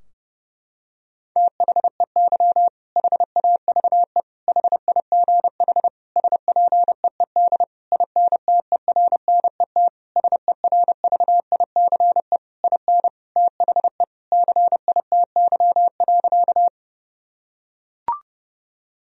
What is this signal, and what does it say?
Morse code, heard as a keyed tone, about 32 wpm, 700 Hz. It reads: THEY HAVE HIGH SPEED INTERNET SERVICE IN THE CITY.
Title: Sentences from Top 1000 Words - Review Part 4 - 30wpm